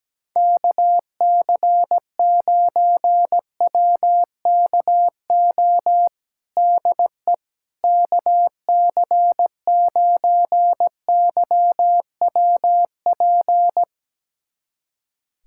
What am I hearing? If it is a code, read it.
Morse code, heard as a keyed tone, about 17 wpm, 700 Hz. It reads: KC9WKO DE KC9YWP
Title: USA Callsign DE Callsign 17wpm